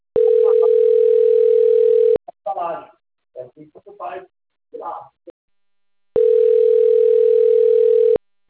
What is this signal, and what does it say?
Ringback tone (North America), 440 + 480 Hz, with speech.
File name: rg-1600-unknown- - - .69099